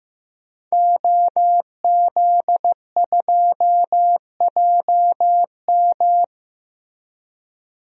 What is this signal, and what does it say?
Morse code, heard as a keyed tone, about 15 wpm, 700 Hz. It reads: OZ2JM
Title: DXCC-100 Call Signs - 5 Characters - Repeated 3x - 15wpm